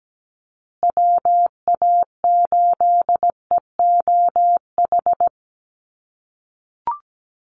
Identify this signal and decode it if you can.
Morse code, heard as a keyed tone, about 17 wpm, 700 Hz. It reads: WA8EOH
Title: USA Call Signs Repeated 2x 17wpm